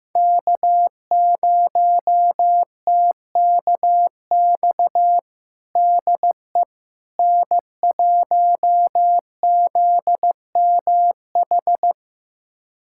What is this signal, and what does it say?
Morse code, heard as a keyed tone, about 15 wpm, 700 Hz. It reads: K0TKX DE N1ZMH
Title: USA Callsign DE Callsign - 5 Characters - 15wpm